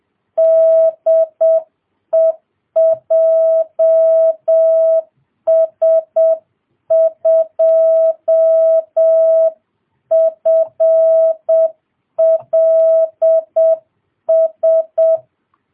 Morse code, heard as a keyed tone, about 7 wpm, 650 Hz. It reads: DE JS2FLS